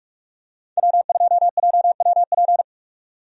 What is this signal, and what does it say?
Morse code, heard as a keyed tone, about 46 wpm, 700 Hz. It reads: W2JWP
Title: USA Call Signs - 5 Characters - 45wpm